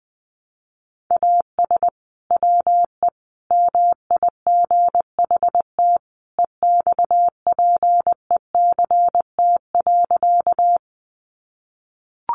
Morse code, heard as a keyed tone, about 20 wpm, 700 Hz. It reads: AS WE MIGHT EXPECT.
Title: Sentences from Top 1100 Words 20wpm